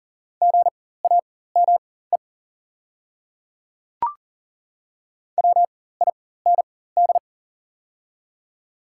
Morse code, heard as a keyed tone, about 40 wpm, spaced as for 19 wpm, 690 Hz. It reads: GAME WIND
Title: 4x Character Spacing - Top 500 Words - 40wpm